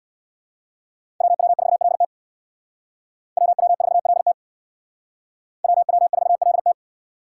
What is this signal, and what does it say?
Morse code, heard as a keyed tone, about 62 wpm, 700 Hz. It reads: WW4LT WW4LT WW4LT
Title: Speed Racing - USA 2x2 Call Signs 50wpm